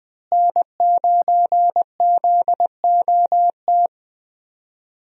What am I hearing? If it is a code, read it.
Morse code, heard as a keyed tone, about 20 wpm, 700 Hz. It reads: N9ZOT
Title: USA Call Signs - 5 Characters - Repeated 2x - 20wpm